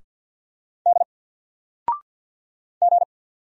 Morse code, heard as a keyed tone, about 50 wpm, 700 Hz. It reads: D G
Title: Single Letter-Number 50wpm